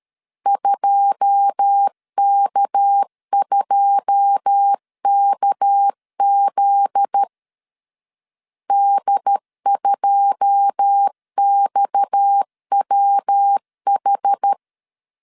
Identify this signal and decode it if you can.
Morse code, heard as a keyed tone, about 13 wpm, 800 Hz. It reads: 2K2KZ D2XWH
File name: CW / DET / 1 9 1500C S C 12wpm 3811